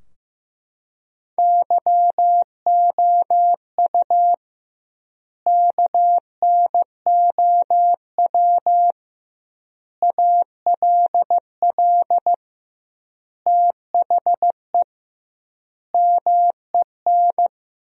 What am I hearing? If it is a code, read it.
Morse code, heard as a keyed tone, about 15 wpm, 700 Hz. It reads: YOU KNOW ALL THE MEN
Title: 2x Word-Spacing - Sentences from Top 300 Words - 15wpm